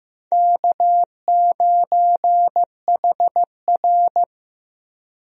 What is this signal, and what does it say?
Morse code, heard as a keyed tone, about 15 wpm, 700 Hz. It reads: K9HR
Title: USA Call Signs - 4 Characters - Repeated 3x - 15wpm